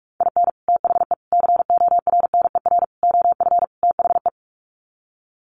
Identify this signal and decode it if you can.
Morse code, heard as a keyed tone, about 44 wpm, 700 Hz. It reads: IN THE CORNER OF THE
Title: Sets of 5 Words Encore 45wpm